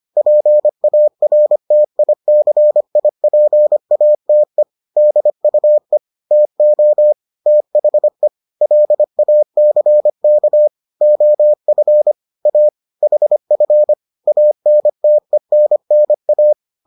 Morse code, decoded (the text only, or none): PARTICIPATE DUE TO THE LACK OF A HF ANTENNA